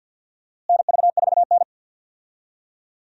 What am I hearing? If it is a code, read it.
Morse code, heard as a keyed tone, about 52 wpm, 700 Hz. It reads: NV4N
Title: USA Call Signs - 4 Characters - Repeated 2x - 50wpm